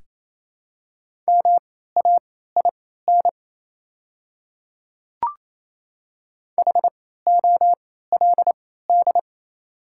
Morse code, heard as a keyed tone, about 28 wpm, spaced as for 16 wpm, 690 Hz. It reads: MAIN HOLD
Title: 3x Character Spacing - Top 500 Words - 28wpm